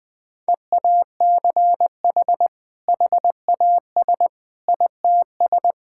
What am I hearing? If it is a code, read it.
Morse code, heard as a keyed tone, about 21 wpm, 700 Hz. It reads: EACH HAS ITS
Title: Sentences from Top 1000 Words - Review Part 5 - 20wpm